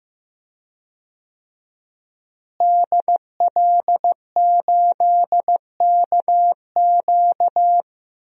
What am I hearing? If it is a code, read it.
Morse code, heard as a keyed tone, about 15 wpm, 700 Hz. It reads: DL8KQ
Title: DXCC-100 Call Signs - 5 Characters - Repeated 3x - 15wpm